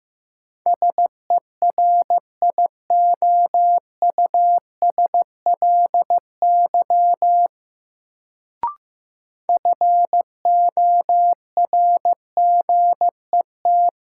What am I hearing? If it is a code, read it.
Morse code, heard as a keyed tone, about 15 wpm, 700 Hz. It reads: SERIOUSLY FORGET
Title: Sentences from Top 1200 Words - New Word Drill - 15wpm